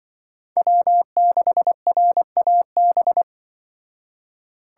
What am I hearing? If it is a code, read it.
Morse code, heard as a keyed tone, about 25 wpm, 700 Hz. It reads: W6RAB
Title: Speed Racing - USA Call Signs 20wpm